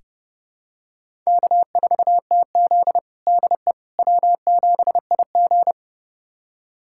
Morse code, heard as a keyed tone, about 30 wpm, 700 Hz. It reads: K4TZ DE W7IG
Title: USA Callsign DE Callsign - 4 Characters - 30wpm